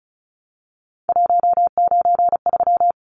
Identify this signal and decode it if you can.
Morse code, heard as a keyed tone, about 34 wpm, 700 Hz. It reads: 193
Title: Numbers - 3-Digit 35wpm